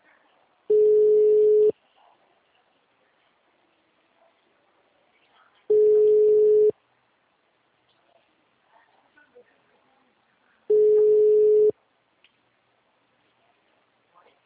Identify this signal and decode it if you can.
Ringback tone (Europe, 425 Hz).